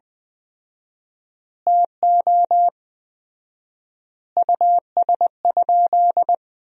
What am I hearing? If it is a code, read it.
Morse code, heard as a keyed tone, about 21 wpm, 700 Hz. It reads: TO US?
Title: 4x Word-Spacing - Sentences from Top 400 Words - 20wpm